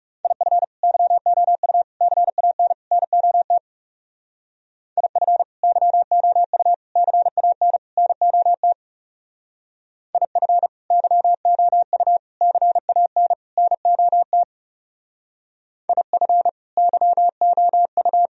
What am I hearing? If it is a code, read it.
Morse code, heard as a keyed tone, about 36 wpm, 700 Hz. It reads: IF YOU CAN NOT IF YOU CAN NOT IF YOU CAN NOT IF YOU
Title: Speed-Racing - Sets of 4 Words 30wpm